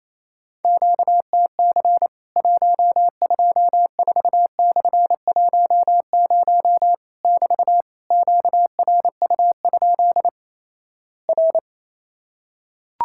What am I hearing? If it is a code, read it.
Morse code, heard as a keyed tone, about 28 wpm, 700 Hz. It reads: QTC 124/10 <BT> QRU? R
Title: Worked All Europe DX Contest - QTC Exchange - 28wpm